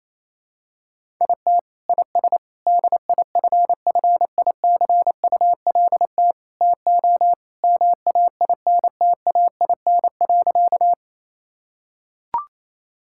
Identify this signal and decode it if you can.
Morse code, heard as a keyed tone, about 28 wpm, 700 Hz. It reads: IT IS DIFFICULT TO MAINTAIN.
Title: Sentences from Top 1000 Words - Review Part 5 - 28wpm